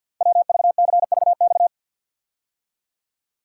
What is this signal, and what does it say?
Morse code, heard as a keyed tone, about 52 wpm, 700 Hz. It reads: WVCVX
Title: Code Groups - 5 Letter:Numbers - 50wpm